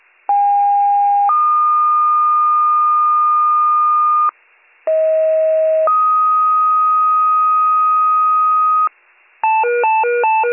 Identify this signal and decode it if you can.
Two-tone page, heard as a one-second tone, then a three-second tone, about 810 then 1220 Hz, about 640 then 1250 Hz. It is heard from 0.3 to 8.9 s.